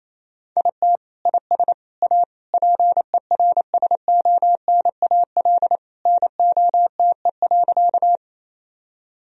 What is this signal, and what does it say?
Morse code, heard as a keyed tone, about 28 wpm, 700 Hz. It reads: IT IS A PERSONAL NOTE.
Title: Sentences from Top 1000 Words - Review Part 4 - 28wpm